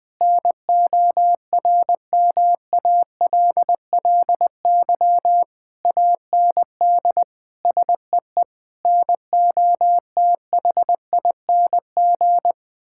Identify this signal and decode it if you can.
Morse code, heard as a keyed tone, about 21 wpm, 700 Hz. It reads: NORMALLY AND SEE NOTHING
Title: Sentences from Top 1000 Words - Review Part 6 - 20wpm